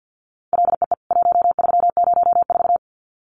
Morse code, heard as a keyed tone, about 50 wpm, 700 Hz. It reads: LEE 1304
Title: CWT Exchange 50wpm